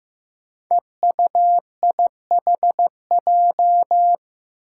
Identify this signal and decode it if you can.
Morse code, heard as a keyed tone, about 16 wpm, 700 Hz. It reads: EUIHJ
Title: Five Challenging Letter-Number Combinations 15wpm